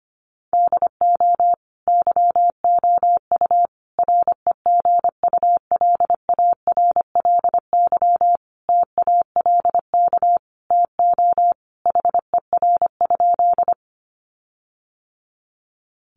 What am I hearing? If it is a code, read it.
Morse code, heard as a keyed tone, about 25 wpm, 700 Hz. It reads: DO YOU REGULARLY TALK TO HER?
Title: Sentences from Top 1400 Words 25wpm